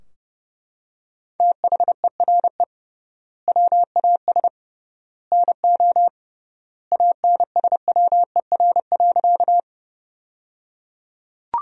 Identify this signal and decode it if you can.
Morse code, heard as a keyed tone, about 29 wpm, 700 Hz. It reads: THERE WAS NO ANSWER.
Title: 3x Word-Spacing - Sentences from Top 300 Words - 30wpm